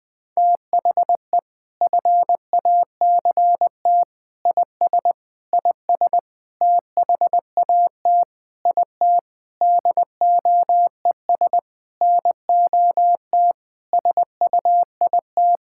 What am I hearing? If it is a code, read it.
Morse code, heard as a keyed tone, about 20 wpm, 700 Hz. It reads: THE FACT IS IS THAT IT DOES NOT SUIT
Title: Sentences from Top 1600 Words 20wpm